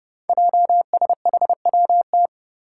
Morse code, heard as a keyed tone, about 28 wpm, 700 Hz. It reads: JSHWT
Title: Five Challenging Letter-Number Combinations 30wpm